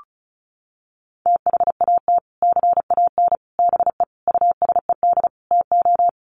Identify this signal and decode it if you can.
Morse code, heard as a keyed tone, about 34 wpm, 700 Hz. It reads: THAT CAN BE USED TO
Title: Sets of 5 Words Encore 35wpm